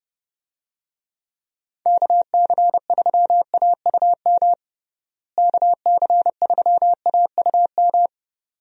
Morse code, heard as a keyed tone, about 30 wpm, 700 Hz. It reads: KC3AUM KC3AUM
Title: USA Call Signs - 6 Characters - Repeated 2x - 30wpm